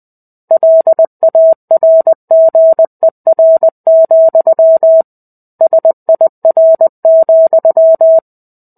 Morse code, heard as a keyed tone, about 20 wpm, 650 Hz. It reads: LARGER, SIR,
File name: 05-Chapter-05 20 WPM